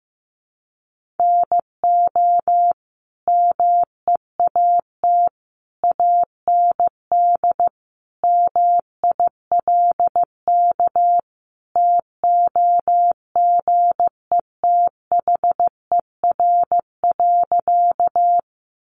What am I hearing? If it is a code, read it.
Morse code, heard as a keyed tone, about 15 wpm, 700 Hz. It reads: NO MEAT AND MILK TOGETHER.
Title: Sentences from Top 1500 Words 15wpm